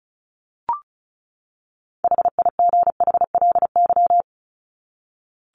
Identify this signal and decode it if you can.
Morse code, heard as a keyed tone, about 34 wpm, 700 Hz. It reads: HIGHLY
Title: LY Suffix - Top 100 Words with Base Words 35wpm